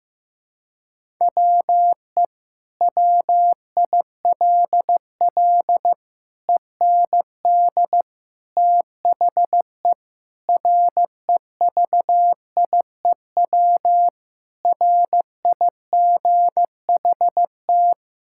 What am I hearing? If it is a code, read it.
Morse code, heard as a keyed tone, about 15 wpm, 700 Hz. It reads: WE WILL END THE REVIEW RIGHT
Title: Sentences from Top 1000 Words - Review Part 6 - 15wpm